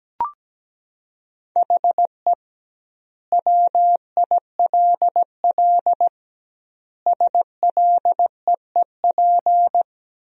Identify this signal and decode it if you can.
Morse code, heard as a keyed tone, about 17 wpm, 700 Hz. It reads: HE WILL SLEEP
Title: 2x Word-Spacing - Sentences from Top 500 Words - 17wpm